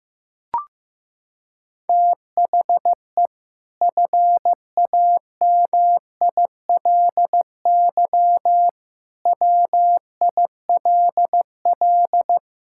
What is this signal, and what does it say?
Morse code, heard as a keyed tone, about 16 wpm, 700 Hz. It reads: THE FAMILY WILL
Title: Sentences from Top 1000 Words - Review Part 6 - 15wpm